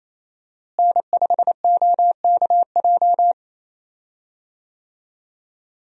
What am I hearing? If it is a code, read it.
Morse code, heard as a keyed tone, about 28 wpm, 700 Hz. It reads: N5OKJ
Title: USA Call Signs - 5 Characters - Repeated 2x - 28wpm